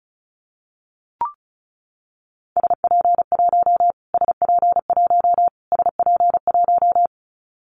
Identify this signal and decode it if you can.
Morse code, heard as a keyed tone, about 34 wpm, 700 Hz. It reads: SP1 SP1 SP1
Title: Callsign Prefixes for Top 25 DXCC - Repeated 3x - 35wpm